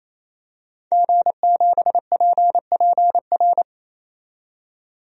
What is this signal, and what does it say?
Morse code, heard as a keyed tone, about 28 wpm, 700 Hz. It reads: G7PPR
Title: DXCC-100 Call Signs - 5 Characters - 28wpm